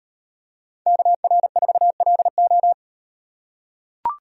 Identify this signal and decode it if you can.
Morse code, heard as a keyed tone, about 35 wpm, 700 Hz. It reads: KR4LO